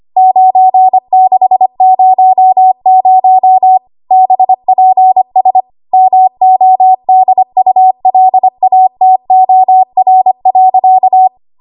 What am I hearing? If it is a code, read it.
Morse code, heard as a keyed tone, about 25 wpm, 750 Hz. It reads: 9600 BPS MODULATOR.
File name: Archive / 25 WPM / 25WPM